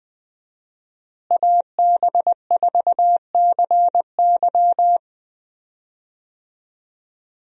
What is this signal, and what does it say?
Morse code, heard as a keyed tone, about 20 wpm, 700 Hz. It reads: AB4CY
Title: USA Call Signs - 5 Characters - Repeated 2x - 20wpm